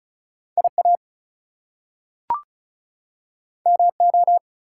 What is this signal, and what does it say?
Morse code, heard as a keyed tone, about 36 wpm, 700 Hz. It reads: IA MO
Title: US States and Canadian Provinces 35wpm